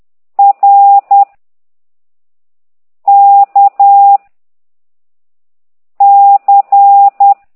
Morse code, heard as a keyed tone, about 10 wpm, 800 Hz. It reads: R K C